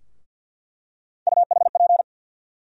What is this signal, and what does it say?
Morse code, heard as a keyed tone, about 47 wpm, 700 Hz. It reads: UHP
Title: USA Call Sign Postfixes - 3 Letters 50wpm